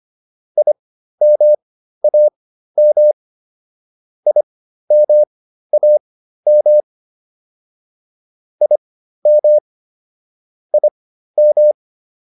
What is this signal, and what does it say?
Morse code, heard as a keyed tone, about 25 wpm, spaced as for 13 wpm, 600 Hz. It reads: IMAM IMAM IM IM